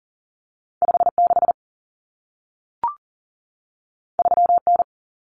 Morse code, heard as a keyed tone, about 40 wpm, 700 Hz. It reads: 56 3N
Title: Common Cut Numbers - 2-Digit 40wpm